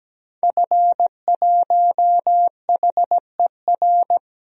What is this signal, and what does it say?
Morse code, heard as a keyed tone, about 17 wpm, 700 Hz. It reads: F1HER
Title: DXCC-100 Call Signs - 5 Characters - 17wpm